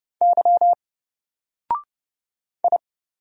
Morse code, heard as a keyed tone, about 30 wpm, 700 Hz. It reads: Y I
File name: Single-Letter-Number-30wpm